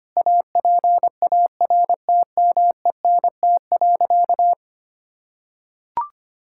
Morse code, heard as a keyed tone, about 25 wpm, 700 Hz. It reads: APARTMENT.
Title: Sentences from Top 1000 Words - Review Part 6 - 25wpm